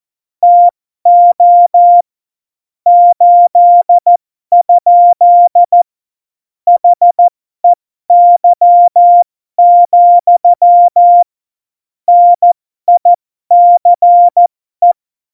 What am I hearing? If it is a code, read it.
Morse code, heard as a keyed tone, about 14 wpm, spaced as for 12 wpm, 700 Hz. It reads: TO 8? HEY, NICE
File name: Morsekurs / Witze